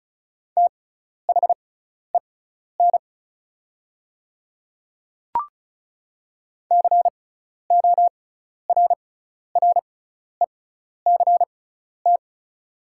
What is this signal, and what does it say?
Morse code, heard as a keyed tone, about 36 wpm, spaced as for 12 wpm, 700 Hz. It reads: THEN CORRECT
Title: 6x Character Spacing - Top 500 Words - 35wpm